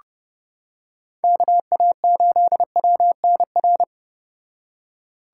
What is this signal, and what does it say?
Morse code, heard as a keyed tone, about 29 wpm, 700 Hz. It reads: KA8WNR